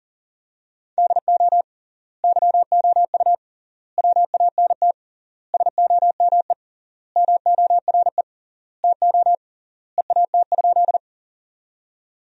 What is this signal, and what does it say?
Morse code, heard as a keyed tone, about 40 wpm, 700 Hz. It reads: DO YOU WANT SOME MORE TO EAT?
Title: 3x Word-Spacing - Sentences from Top 300 Words - 40wpm